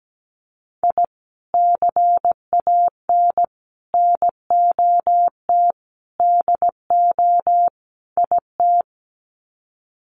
Morse code, heard as a keyed tone, about 17 wpm, 700 Hz. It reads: I CAN NOT DO IT
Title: Sets of 5 Words Encore 17wpm